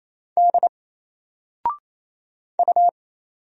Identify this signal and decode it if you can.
Morse code, heard as a keyed tone, about 27 wpm, 700 Hz. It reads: D U
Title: Single Character (Letter:Number:Punctuation:Prosign) 28wpm